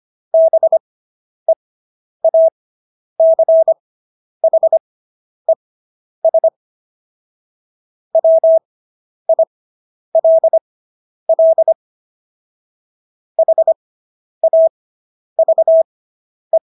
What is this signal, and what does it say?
Morse code, heard as a keyed tone, about 25 wpm, spaced as for 10 wpm, 650 Hz. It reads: BEACHES WILL HAVE